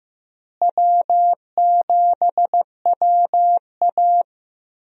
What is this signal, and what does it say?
Morse code, heard as a keyed tone, about 16 wpm, 700 Hz. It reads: W7WA
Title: USA Call Signs - 4 Characters - Repeated 3x - 15wpm